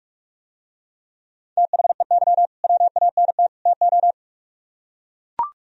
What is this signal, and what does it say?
Morse code, heard as a keyed tone, about 45 wpm, 700 Hz. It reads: THEY WANT TO